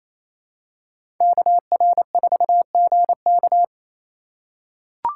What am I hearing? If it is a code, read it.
Morse code, heard as a keyed tone, about 28 wpm, 700 Hz. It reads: KR4GK
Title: USA Call Signs - 5 Characters - 4x Character Spacing - 28wpm